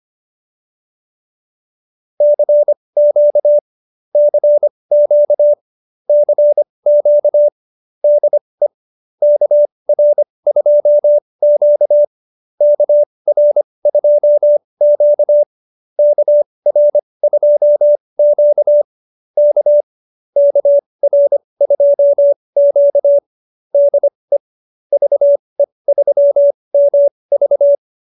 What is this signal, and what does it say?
Morse code, heard as a keyed tone, about 25 wpm, spaced as for 20 wpm, 590 Hz. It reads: CQ CQ CQ DE KR2Q KR2Q KR2Q K KR2Q DE VE3MV